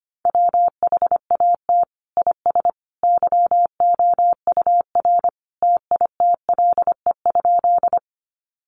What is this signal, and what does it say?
Morse code, heard as a keyed tone, about 25 wpm, 700 Hz. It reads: WHAT IS YOUR TITLE?
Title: Sentences from Top 1200 Words 25wpm